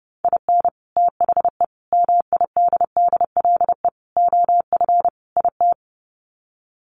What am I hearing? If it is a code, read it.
Morse code, heard as a keyed tone, about 28 wpm, 700 Hz. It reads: IN THE MIDDLE OF IT
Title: Sets of 5 Words Encore 30wpm